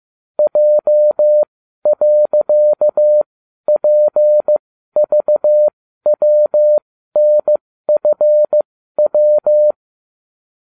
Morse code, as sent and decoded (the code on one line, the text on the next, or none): .--- .-.-.- .--. ...- .-- -. ..-. .--
J.PVWNFW